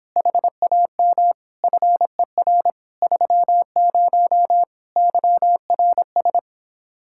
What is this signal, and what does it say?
Morse code, heard as a keyed tone, about 26 wpm, 700 Hz. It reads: HAM FER 30 YRS